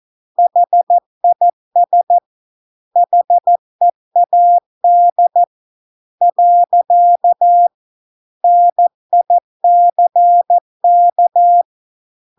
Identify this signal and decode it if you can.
Morse code, heard as a keyed tone, about 14 wpm, 700 Hz. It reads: HIS HEAD . NICK